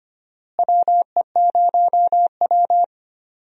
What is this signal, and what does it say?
Morse code, heard as a keyed tone, about 25 wpm, 700 Hz. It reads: WE0W
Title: USA Call Signs - 4 Characters - Repeated 3x - 25wpm